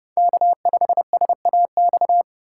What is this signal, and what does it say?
Morse code, heard as a keyed tone, about 30 wpm, 700 Hz. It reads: K5SAX
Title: USA Call Signs - 3x Character Spacing - 30wpm